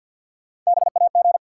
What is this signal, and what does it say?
Morse code, heard as a keyed tone, about 48 wpm, 700 Hz. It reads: BAG
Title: Three Letter Words - 50wpm